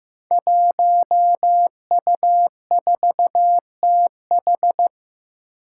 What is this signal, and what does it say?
Morse code, heard as a keyed tone, about 15 wpm, 700 Hz. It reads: 1U4TH
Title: Five Challenging Letter-Number Combinations 15wpm